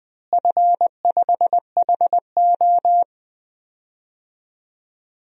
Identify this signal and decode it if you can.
Morse code, heard as a keyed tone, about 20 wpm, 700 Hz. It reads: F5HO
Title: DXCC-100 Call Signs - 4 Characters - Repeated 3x - 20wpm